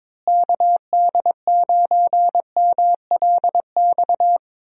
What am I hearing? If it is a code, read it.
Morse code, heard as a keyed tone, about 22 wpm, 700 Hz. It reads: KD9MLX
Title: USA Call Signs - 6 Characters - Repeated 3x - 22wpm